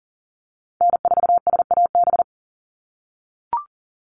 Morse code, heard as a keyed tone, about 40 wpm, 700 Hz. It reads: N4SAB
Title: USA Call Signs 40wpm